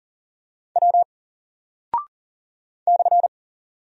Morse code, heard as a keyed tone, about 39 wpm, 700 Hz. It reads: W /
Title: Single Character (Letter:Number:Punctuation:Prosign) 40wpm